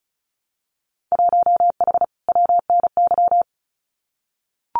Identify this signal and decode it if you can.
Morse code, heard as a keyed tone, about 36 wpm, 700 Hz. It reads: 1H WNY
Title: Winter Field Day Exchanges Encore 35wpm